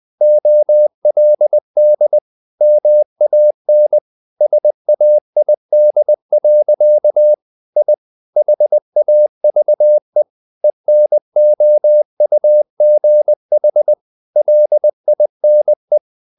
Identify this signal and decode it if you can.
Morse code, heard as a keyed tone, about 20 wpm, 600 Hz. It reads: OLD MAN SAID. I HAVE ENOUGH LINE
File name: the old man and the sea / the old man and the sea part 1 20wpm